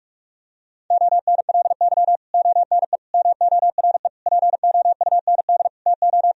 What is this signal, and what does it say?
Morse code, heard as a keyed tone, about 46 wpm, 700 Hz. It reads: ONLY ONE MORE POUND TO
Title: Sentences from Top 1000 Words - Review Part 2 - 45wpm